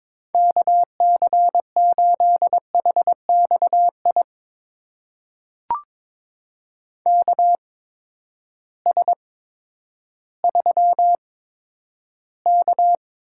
Morse code, heard as a keyed tone, about 22 wpm, spaced as for 11 wpm, 700 Hz. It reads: KC8HXI KS3K
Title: USA Call Signs - 8x Character Spacing - 22wpm